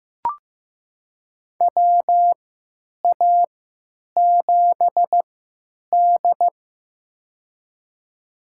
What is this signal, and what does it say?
Morse code, heard as a keyed tone, about 16 wpm, spaced as for 9 wpm, 700 Hz. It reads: WA7D
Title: USA Call Signs - 3x Character Spacing - 15wpm